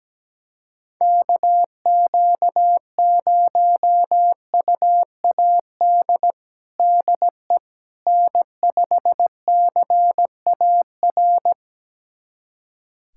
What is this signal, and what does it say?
Morse code, heard as a keyed tone, about 17 wpm, 700 Hz. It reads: KQ0UAD DE N5CAR
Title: USA Callsign DE Callsign 17wpm